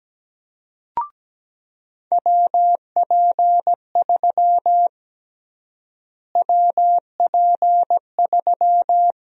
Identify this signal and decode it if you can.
Morse code, heard as a keyed tone, about 17 wpm, 700 Hz. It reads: WP3 WP3
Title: USA Call Sign Prefixes - Repeated 2x - 17wpm